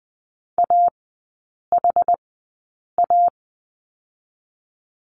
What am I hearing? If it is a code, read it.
Morse code, heard as a keyed tone, about 20 wpm, spaced as for 8 wpm, 700 Hz. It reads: AHA
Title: 074 Lesson 13e - Letter L - Three Characters 20wpm